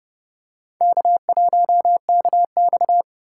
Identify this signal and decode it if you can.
Morse code, heard as a keyed tone, about 30 wpm, 700 Hz. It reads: K1KX
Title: USA Call Signs - 4 Characters - 4x Character Spacing - 30wpm